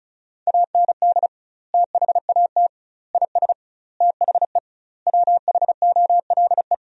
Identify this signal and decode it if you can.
Morse code, heard as a keyed tone, about 34 wpm, 700 Hz. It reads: AND THAT IS THE WHOLE
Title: 2x Word-Spacing - Sentences from Top 400 Words - 35wpm